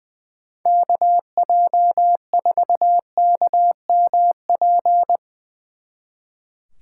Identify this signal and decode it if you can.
Morse code, heard as a keyed tone, about 20 wpm, 700 Hz. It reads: KJ4KMP
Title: Speed Racing - USA 2x3 Call Signs 20wpm